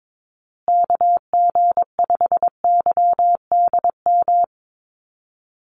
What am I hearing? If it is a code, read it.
Morse code, heard as a keyed tone, about 22 wpm, 700 Hz. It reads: KG5YDM
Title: USA Call Signs Repeated 3x 22wpm